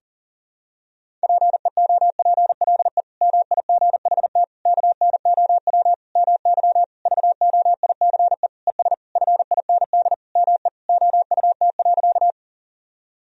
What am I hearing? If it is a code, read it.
Morse code, heard as a keyed tone, about 40 wpm, 700 Hz. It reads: PEOPLE MIGHT KNOW MY VOICE ES FIND ME OUT.
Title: Sentences from Top 500 Words and Abbreviations 40wpm